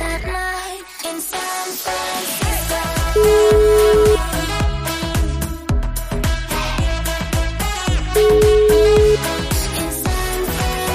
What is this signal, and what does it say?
Ringback tone (Europe, 425 Hz), with noise.